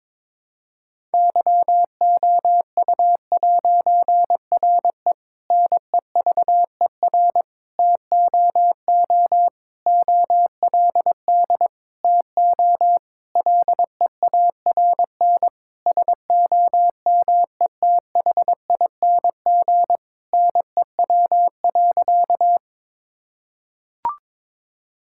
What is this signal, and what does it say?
Morse code, heard as a keyed tone, about 22 wpm, 700 Hz. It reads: YOU'RE NEVER TOO OLD TO LEARN SOMETHING NEW.
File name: Difficult Sentences with Hints - 22wpm